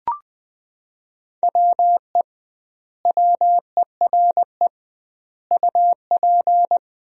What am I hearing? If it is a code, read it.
Morse code, heard as a keyed tone, about 20 wpm, 700 Hz. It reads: WE WERE UP
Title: 2x Word-Spacing - Sentences from Top 400 Words - 20wpm